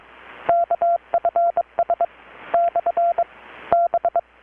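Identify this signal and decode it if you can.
Morse code, heard as a keyed tone, about 22 wpm, 680 Hz, with radio noise. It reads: KFS / B
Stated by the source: Morse from KPH's Night of Nights XX maritime radiotelegraphy event, received in Chiapas, Mexico about 3,800 km away on a Kenwood R-600 with a grounded 10 m long-wire antenna.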